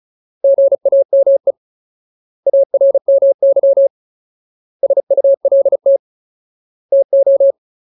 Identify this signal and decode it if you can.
Morse code, heard as a keyed tone, about 34 wpm, 550 Hz. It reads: GAME ARMY SULT TO
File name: Ultimate Word List / 35WPM Ultimate